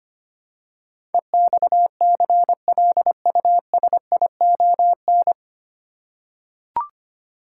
Morse code, heard as a keyed tone, about 25 wpm, 700 Hz. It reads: EXCLUSION
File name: practice / 25wpm / 9 Letter Words - First 3 Letters Given - 25wpm